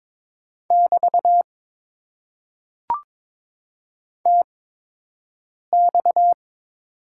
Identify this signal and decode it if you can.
Morse code, heard as a keyed tone, about 22 wpm, spaced as for 6 wpm, 700 Hz. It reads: <BT> TX
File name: Top 90 QSO Elements - 8x Character Spacing - 22wpm